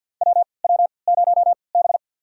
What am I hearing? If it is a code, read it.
Morse code, heard as a keyed tone, about 52 wpm, spaced as for 29 wpm, 700 Hz. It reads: WW0B
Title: USA Call Signs - 4 Characters - 3x Character Spacing - 50wpm